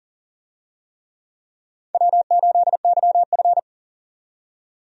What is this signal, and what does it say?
Morse code, heard as a keyed tone, about 40 wpm, 700 Hz. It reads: W8YF
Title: USA Call Signs - 4 Characters - 40wpm